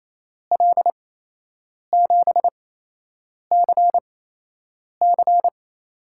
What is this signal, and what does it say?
Morse code, heard as a keyed tone, about 28 wpm, spaced as for 8 wpm, 700 Hz. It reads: L7CC
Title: USA Call Signs - 8x Character Spacing - 28wpm